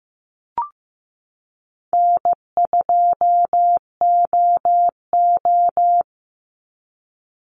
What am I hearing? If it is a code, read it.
Morse code, heard as a keyed tone, about 15 wpm, 700 Hz. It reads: N2OO
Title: Easy Call Signs Repeated 2x - 15wpm